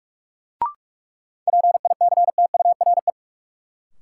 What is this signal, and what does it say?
Morse code, heard as a keyed tone, about 46 wpm, 700 Hz. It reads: PICTURE
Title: Sentences from Top 500 Words - New Word Drill - 45wpm